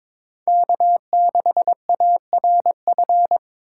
Morse code, heard as a keyed tone, about 22 wpm, 700 Hz. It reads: K6ARF
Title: USA Call Signs - 5 Characters - 3x Character Spacing - 22wpm